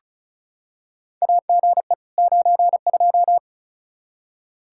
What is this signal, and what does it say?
Morse code, heard as a keyed tone, about 34 wpm, 700 Hz. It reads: AGE 92
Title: Practice QSO - Exchange 3 - More Info - 35wpm